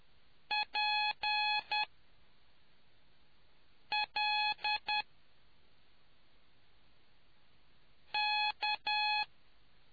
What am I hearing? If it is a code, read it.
Morse code, heard as a keyed tone, about 10 wpm, spaced as for 3 wpm, 840 Hz. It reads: PLK